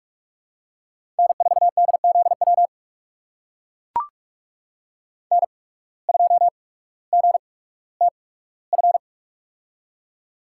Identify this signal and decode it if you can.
Morse code, heard as a keyed tone, about 44 wpm, spaced as for 19 wpm, 700 Hz. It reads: N4DZW N2GTF
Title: USA Call Signs - 5 Characters - 8x Character Spacing - 45wpm